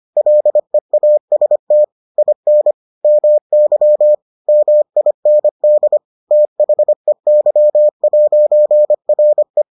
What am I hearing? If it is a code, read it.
Morse code, heard as a keyed tone, about 25 wpm, 600 Hz. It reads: LEAST IN MY MIND THEY'RE